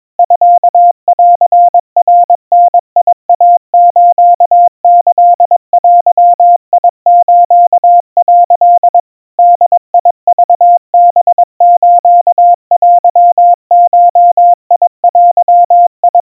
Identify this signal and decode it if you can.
Morse code (Japanese Wabun code), heard as a keyed tone, about 22 wpm, 700 Hz. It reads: ミンナダイスキデス」 ボクハステゴデ
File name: 80 cpm 700 Hz Crispy